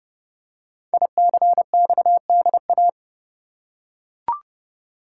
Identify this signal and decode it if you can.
Morse code, heard as a keyed tone, about 28 wpm, 700 Hz. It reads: ICXDA
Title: Code Groups - 5 Letter:Numbers - 30wpm